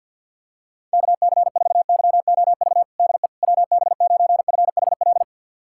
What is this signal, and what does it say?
Morse code, heard as a keyed tone, about 51 wpm, 700 Hz. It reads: KC4YOV DE WB9FHL
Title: USA Callsign DE Callsign - 6 Characters - 50wpm